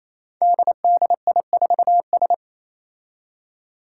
Morse code, heard as a keyed tone, about 29 wpm, 700 Hz. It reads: DDI4S
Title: Five Challenging Letter-Number Combinations 28wpm